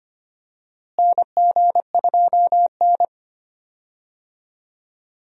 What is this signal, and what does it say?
Morse code, heard as a keyed tone, about 25 wpm, 700 Hz. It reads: NG2N
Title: USA Call Signs - 4 Characters - Repeated 2x - 25wpm